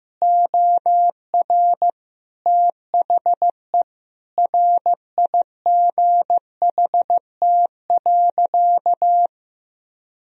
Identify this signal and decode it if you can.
Morse code, heard as a keyed tone, about 15 wpm, 700 Hz. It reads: OR THE RIGHT.
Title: Sentences from Top 1000 Words - Review Part 4 - 15wpm